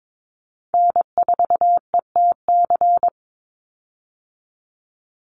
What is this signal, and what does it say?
Morse code, heard as a keyed tone, about 22 wpm, 700 Hz. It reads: N4ETC
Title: Call Signs - 22wpm